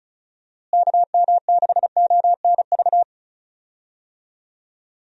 Morse code, heard as a keyed tone, about 34 wpm, 700 Hz. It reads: KM6ONV
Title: USA Call Signs - 6 Characters - Repeated 3x - 35wpm